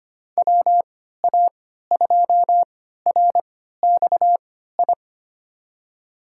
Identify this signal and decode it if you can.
Morse code, heard as a keyed tone, about 25 wpm, spaced as for 14 wpm, 700 Hz. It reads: WA2RXI
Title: USA Call Signs - 6 Characters - 3x Character Spacing - 25wpm